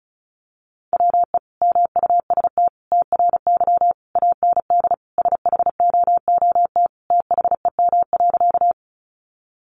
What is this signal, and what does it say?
Morse code, heard as a keyed tone, about 34 wpm, 700 Hz. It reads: WE MUST TRY AND SHOOT THEM.
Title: Sentences from Top 1200 Words 35wpm